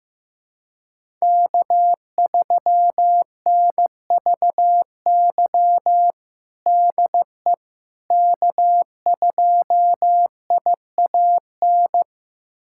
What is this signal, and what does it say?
Morse code, heard as a keyed tone, about 15 wpm, 700 Hz. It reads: K3NVY DE K2IAN
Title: USA Callsign DE Callsign - 5 Characters - 15wpm